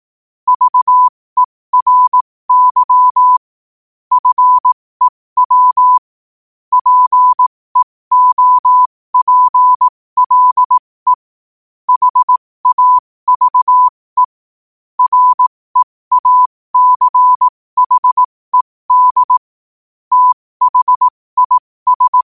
Morse code, heard as a keyed tone, about 18 wpm, spaced as for 15 wpm, 1000 Hz. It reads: VERY FEW PEOPLE HAVE REACHED THIS
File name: qotdfiles / 15wpm